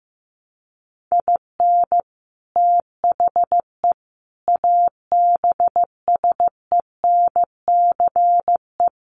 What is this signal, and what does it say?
Morse code, heard as a keyed tone, about 15 wpm, 700 Hz. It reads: IN THE ABSENCE
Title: Sets of 5 Words Encore 15wpm